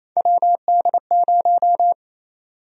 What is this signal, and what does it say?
Morse code, heard as a keyed tone, about 28 wpm, 700 Hz. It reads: WD0
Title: USA Call Sign Prefixes - 28wpm